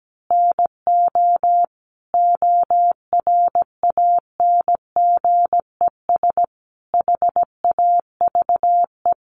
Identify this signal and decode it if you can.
Morse code, heard as a keyed tone, about 17 wpm, 700 Hz. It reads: NO ORANGES HAVE
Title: Easy Phrases 17wpm